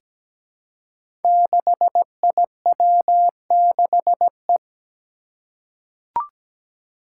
Morse code, heard as a keyed tone, about 17 wpm, 700 Hz. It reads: 6IW6E
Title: Five Challenging Letter-Number Combinations 17wpm